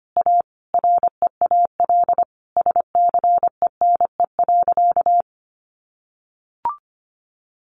Morse code, heard as a keyed tone, about 25 wpm, 700 Hz. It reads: A REAL SCENE.
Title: Sentences from Top 1100 Words 25wpm